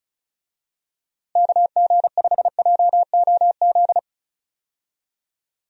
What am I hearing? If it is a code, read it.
Morse code, heard as a keyed tone, about 34 wpm, 700 Hz. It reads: KG5JOZ
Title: USA Call Signs - 6 Characters - Repeated 2x - 35wpm